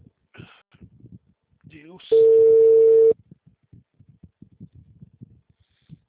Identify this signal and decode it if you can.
Ringback tone (China, 450 Hz), with speech.
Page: Capital - SP